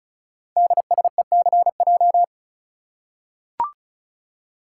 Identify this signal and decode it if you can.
Morse code, heard as a keyed tone, about 36 wpm, 690 Hz. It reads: DSECJ
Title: Code Groups - 5 Letters - 35wpm